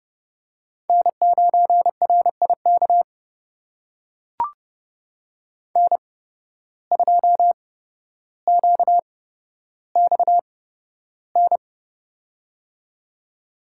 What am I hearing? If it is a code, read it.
Morse code, heard as a keyed tone, about 30 wpm, spaced as for 13 wpm, 700 Hz. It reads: N9RIK N2QXN
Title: USA Call Signs - 5 Characters - 8x Character Spacing - 30wpm